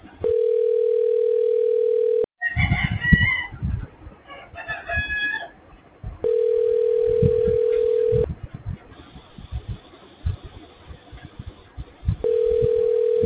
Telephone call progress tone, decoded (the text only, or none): ringback tone